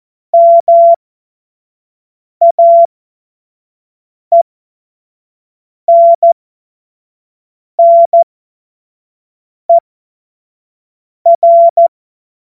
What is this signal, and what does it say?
Morse code, heard as a keyed tone, about 14 wpm, spaced as for 5 wpm, 680 Hz. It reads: MAENNER
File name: Morsekurs / Witze / Witz 17. Januar 2023 25 F70